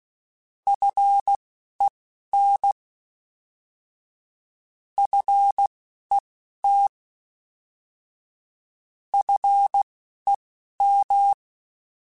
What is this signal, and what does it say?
Morse code, heard as a keyed tone, about 16 wpm, spaced as for 11 wpm, 800 Hz. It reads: FEN FET FEM
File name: Alphabetical / Alphabetical mp3s / Fs @10